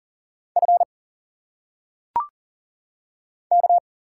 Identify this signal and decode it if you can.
Morse code, heard as a keyed tone, about 40 wpm, 700 Hz. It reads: F K
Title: Single Letters - Phonetic Pronunciation 40wpm